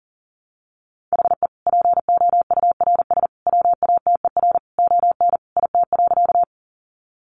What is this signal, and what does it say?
Morse code, heard as a keyed tone, about 40 wpm, 700 Hz. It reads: HE POURS WATER ON IT.
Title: Additional Easy Phrases 40wpm